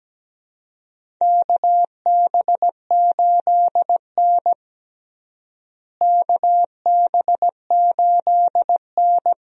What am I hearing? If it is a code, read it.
Morse code, heard as a keyed tone, about 17 wpm, 700 Hz. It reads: KB8N KB8N
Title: USA Call Signs - 4 Characters - Repeated 3x - 17wpm